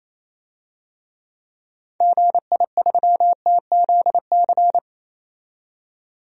Morse code, heard as a keyed tone, about 28 wpm, 700 Hz. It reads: GI3TZC
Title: DXCC-100 Call Signs - 6 Characters - 28wpm